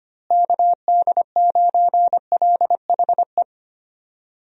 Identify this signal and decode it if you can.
Morse code, heard as a keyed tone, about 25 wpm, 700 Hz. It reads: KD9LHE
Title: USA Call Signs - 6 Characters - Repeated 2x - 25wpm